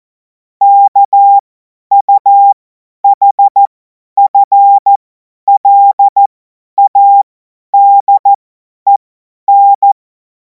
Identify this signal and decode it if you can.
Morse code, heard as a keyed tone, about 14 wpm, spaced as for 10 wpm, 800 Hz. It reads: KUHFLADEN
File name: Morsekurs / Witze / Witz 21. April 2026 50 F70